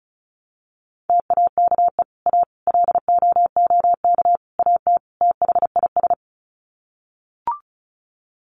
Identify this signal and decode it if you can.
Morse code, heard as a keyed tone, about 34 wpm, 700 Hz. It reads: TAKE A LOOK AT THIS
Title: Sets of 5 Words 35wpm